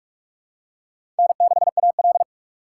Morse code, heard as a keyed tone, about 46 wpm, 700 Hz. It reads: N6AL
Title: USA Call Signs - 4 Characters - 2x Character Spacing - 45wpm